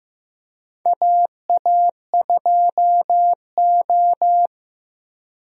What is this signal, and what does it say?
Morse code, heard as a keyed tone, about 15 wpm, 700 Hz. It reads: AA2O
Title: USA Call Signs - 4 Characters - 15wpm